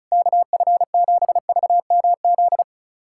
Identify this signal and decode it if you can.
Morse code, heard as a keyed tone, about 36 wpm, 700 Hz. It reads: KF7VMZ
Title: USA Callsign DE Callsign - 6 Characters - 35wpm